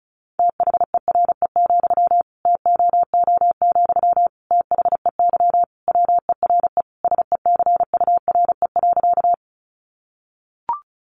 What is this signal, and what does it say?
Morse code, heard as a keyed tone, about 34 wpm, 700 Hz. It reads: THERE, TOO, THEY WERE SECURE.
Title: Sentences from Top 1200 Words 35wpm